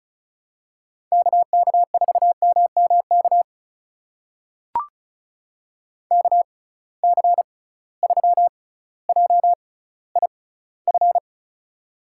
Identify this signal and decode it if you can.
Morse code, heard as a keyed tone, about 36 wpm, spaced as for 18 wpm, 690 Hz. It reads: KK4MMK KC3JIF
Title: USA Call Signs - 6 Characters - 6x Character Spacing - 35wpm